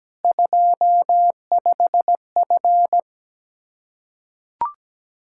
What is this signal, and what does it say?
Morse code, heard as a keyed tone, about 17 wpm, 700 Hz. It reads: 25F
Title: Practice QSO - Exchange 2 - Info - 17wpm